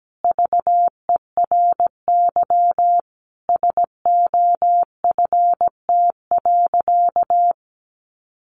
Morse code, heard as a keyed tone, about 17 wpm, 700 Hz. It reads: VERY SOFT.
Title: Easy Phrases 17wpm